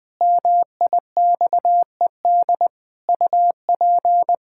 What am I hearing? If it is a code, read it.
Morse code, heard as a keyed tone, about 20 wpm, 700 Hz. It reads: MIXED UP
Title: Pangram Challenge v2 - 20wpm